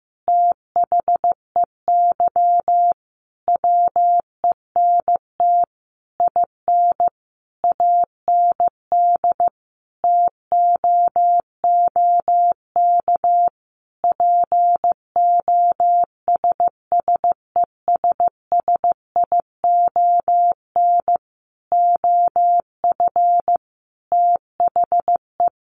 Morse code, heard as a keyed tone, about 15 wpm, 700 Hz. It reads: THEY WENT IN AND TOOK POSSESSION OF THE
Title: Sentences from Top 1400 Words 15wpm